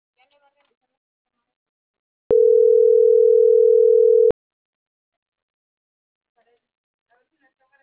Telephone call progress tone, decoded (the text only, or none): ringback tone